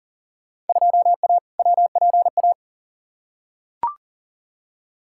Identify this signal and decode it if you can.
Morse code, heard as a keyed tone, about 40 wpm, 700 Hz. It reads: 2A WPA
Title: Field Day Exchanges - 40wpm